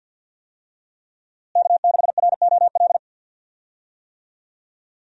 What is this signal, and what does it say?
Morse code, heard as a keyed tone, about 49 wpm, 700 Hz. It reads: K6ROL
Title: USA Call Signs - 5 Characters - Repeated 2x - 50wpm